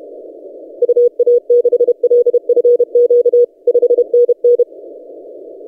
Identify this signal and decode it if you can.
Morse code, heard as a keyed tone, about 30 wpm, 480 Hz. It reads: UA6LFQ 5NN
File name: posts / ri0x-20cw- -0519z